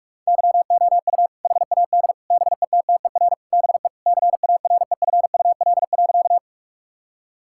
Morse code, heard as a keyed tone, about 46 wpm, 700 Hz. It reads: YOU HAD BETTER BE CAREFUL.
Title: Sentences from Top 1600 Words 45wpm